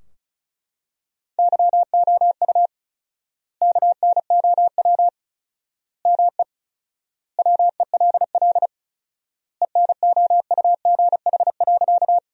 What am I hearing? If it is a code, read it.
Morse code, heard as a keyed tone, about 36 wpm, 700 Hz. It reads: YOU KNOW ME WELL ENOUGH.
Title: 4x Word-Spacing - Sentences from Top 400 Words - 35wpm